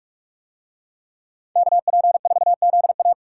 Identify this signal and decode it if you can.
Morse code, heard as a keyed tone, about 45 wpm, 700 Hz. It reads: KP4ZA